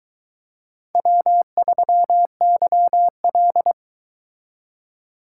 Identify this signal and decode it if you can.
Morse code, heard as a keyed tone, about 22 wpm, 700 Hz. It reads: W3YL